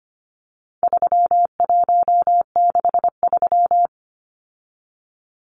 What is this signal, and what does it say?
Morse code, heard as a keyed tone, about 25 wpm, 700 Hz. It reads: 3163
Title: Numbers - 4-Digit 25wpm